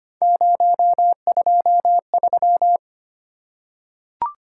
Morse code, heard as a keyed tone, about 25 wpm, 700 Hz. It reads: 023
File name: Three Letter-Number Combinations - 25wpm